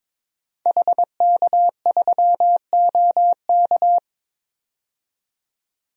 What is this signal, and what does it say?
Morse code, heard as a keyed tone, about 22 wpm, 700 Hz. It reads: HK3OK
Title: DXCC-100 Call Signs - 5 Characters - 22wpm